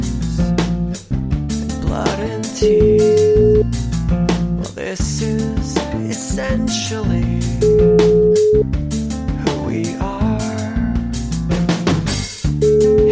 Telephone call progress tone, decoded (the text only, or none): ringback tone